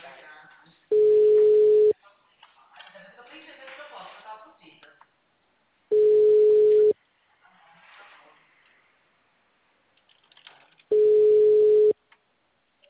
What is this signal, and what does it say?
Ringback tone (Europe, 425 Hz), with speech.
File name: out- -2003- - - .13973